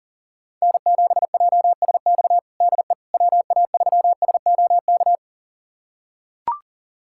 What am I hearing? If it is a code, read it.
Morse code, heard as a keyed tone, about 40 wpm, 700 Hz. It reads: N7JSX DE WA3SOK
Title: USA Callsign DE Callsign 40wpm